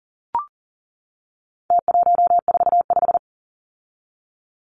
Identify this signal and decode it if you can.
Morse code, heard as a keyed tone, about 40 wpm, 700 Hz. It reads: T145
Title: Numbers - Contest Style - 4-Digit 40wpm